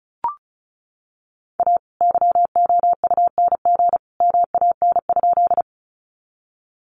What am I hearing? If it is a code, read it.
Morse code, heard as a keyed tone, about 34 wpm, 700 Hz. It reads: A YOUNG MAN?
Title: Sentences from Top 400 Words 35wpm